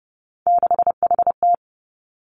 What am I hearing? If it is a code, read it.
Morse code, heard as a keyed tone, about 30 wpm, 700 Hz. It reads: 6HT
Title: Three Challenging Letter-Number Combinations 30wpm